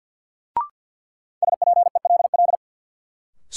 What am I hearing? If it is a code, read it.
Morse code, heard as a keyed tone, about 53 wpm, 700 Hz. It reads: SPELL
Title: Sentences from Top 1000 Words - Review Part 1 - New Word Drill - 50wpm